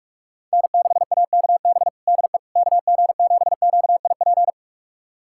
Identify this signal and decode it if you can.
Morse code, heard as a keyed tone, about 42 wpm, 700 Hz. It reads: N6AKB DE KG7QIP
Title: USA Callsign DE Callsign 45wpm